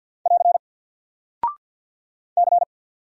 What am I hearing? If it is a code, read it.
Morse code, heard as a keyed tone, about 48 wpm, 700 Hz. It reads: <AR> C
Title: Single Character (Letter:Number:Punctuation:Prosign) 50wpm